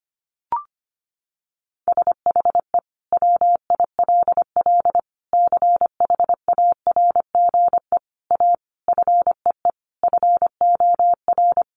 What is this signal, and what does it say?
Morse code, heard as a keyed tone, about 25 wpm, 700 Hz. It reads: SHE WILL CHARGE A FEE FOR
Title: Sentences from Top 1500 Words 25wpm